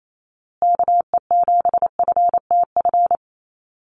Morse code, heard as a keyed tone, about 28 wpm, 700 Hz. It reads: KE7FTF
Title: USA Call Signs Repeated 2x 28wpm